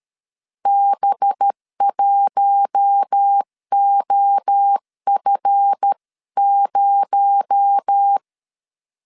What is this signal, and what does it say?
Morse code, heard as a keyed tone, about 13 wpm, 800 Hz. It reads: B1OF0
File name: LEZIONI / 10 SIGNS / 10B INT 12wpm 2414